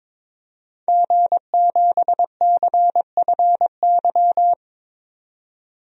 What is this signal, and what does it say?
Morse code, heard as a keyed tone, about 21 wpm, 700 Hz. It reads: G7CFY
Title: DXCC-100 Call Signs - 5 Characters - 22wpm